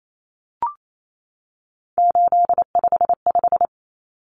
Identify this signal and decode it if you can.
Morse code, heard as a keyed tone, about 28 wpm, 700 Hz. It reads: 855
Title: Numbers - 3-Digit 28wpm